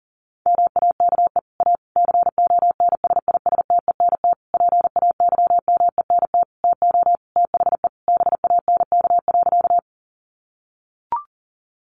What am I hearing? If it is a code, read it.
Morse code, heard as a keyed tone, about 37 wpm, 700 Hz. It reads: MAKE A CONSISTENT PAYMENT TO THE BANK.
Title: Sentences from Top 1500 Words 40wpm